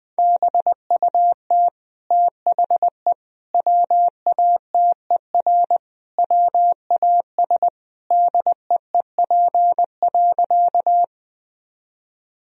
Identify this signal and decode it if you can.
Morse code, heard as a keyed tone, about 21 wpm, 700 Hz. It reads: BUT THE WATER WAS DEEP.
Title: Sentences from Top 1000 Words - Review Part 3 - 20wpm